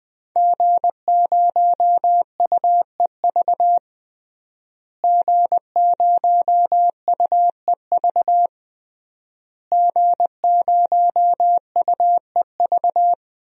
Morse code, heard as a keyed tone, about 20 wpm, 700 Hz. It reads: G0UEV G0UEV G0UEV
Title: DXCC-100 Call Signs - 5 Characters - Repeated 3x - 20wpm